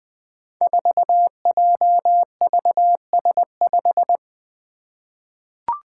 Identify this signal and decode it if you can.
Morse code, heard as a keyed tone, about 20 wpm, 700 Hz. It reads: 4JVS5
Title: Five Challenging Letter-Number Combinations 20wpm